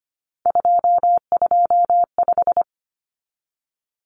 Morse code, heard as a keyed tone, about 25 wpm, 700 Hz. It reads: 225
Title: Common Cut Numbers - 3-Digit 25wpm